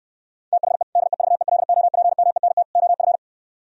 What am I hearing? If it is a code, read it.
Morse code, heard as a keyed tone, about 68 wpm, 700 Hz. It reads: THE DEVELOPMENT OF